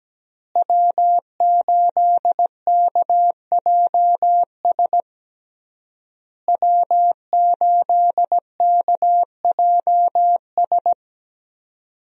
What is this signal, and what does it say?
Morse code, heard as a keyed tone, about 17 wpm, 700 Hz. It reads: W8KJS W8KJS
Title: USA Call Signs - 5 Characters - Repeated 3x - 17wpm